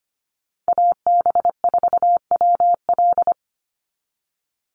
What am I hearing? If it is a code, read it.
Morse code, heard as a keyed tone, about 25 wpm, 700 Hz. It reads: AB4WL
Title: NA SOTA Chaser Callsigns 25wpm